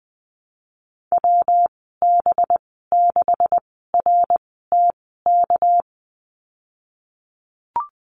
Morse code, heard as a keyed tone, about 20 wpm, spaced as for 15 wpm, 700 Hz. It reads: WB6RTK
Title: 207 Lesson 33g - Letter j - Call Signs 20wpm